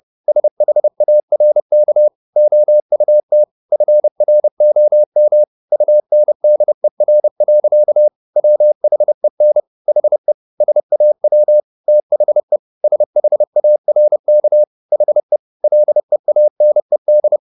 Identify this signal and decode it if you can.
Morse code, heard as a keyed tone, about 30 wpm, 600 Hz. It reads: SHARK OUT FROM UNDER. WHEN HE SAW THE SHARK HE LEANED